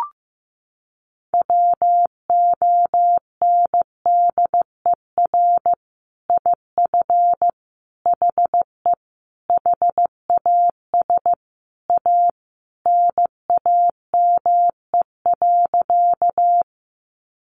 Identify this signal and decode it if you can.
Morse code, heard as a keyed tone, about 15 wpm, 700 Hz. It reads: WONDER IF HE HAS A NAME.
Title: Sentences from Top 500 Words 15wpm